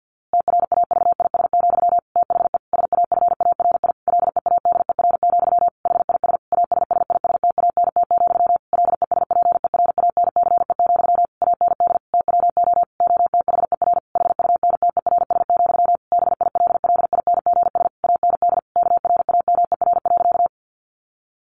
Morse code, heard as a keyed tone, about 50 wpm, 700 Hz. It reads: TRAVIS, THE SAFARI LEADER, HIS ASSISTANT, LESPERANCE, AND TWO OTHER HUNTERS, BILLINGS AND KRAMER.
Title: A Sound Of Thunder 50wpm